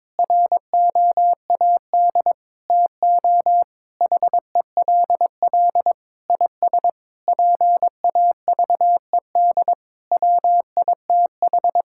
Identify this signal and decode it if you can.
Morse code, heard as a keyed tone, about 22 wpm, 700 Hz. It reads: ROAD TO HELL IS PAVED WITH
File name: practice / 22wpm / Sentences with 8-Word Phrases - 22wpm